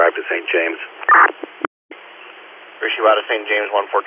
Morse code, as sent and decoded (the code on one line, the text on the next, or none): none